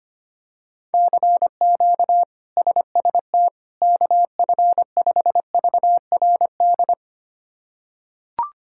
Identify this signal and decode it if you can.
Morse code, heard as a keyed tone, about 25 wpm, 700 Hz. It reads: CQ SST KF5VRD
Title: K1USN SST QSO Practice - 25wpm